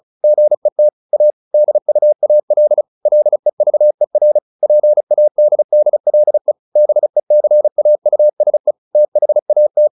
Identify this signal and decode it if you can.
Morse code, heard as a keyed tone, about 34 wpm, 600 Hz. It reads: GET A DUAL LEVER PADDLE BECAUSE THAT